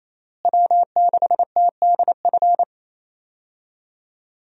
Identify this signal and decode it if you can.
Morse code, heard as a keyed tone, about 27 wpm, 700 Hz. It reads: W6TDF
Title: USA Call Signs - 5 Characters - Repeated 3x - 28wpm